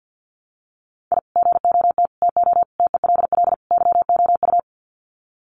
Morse code, heard as a keyed tone, about 50 wpm, 700 Hz. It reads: I GOT TO TELL YOU
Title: Sets of 5 Words Encore2 50wpm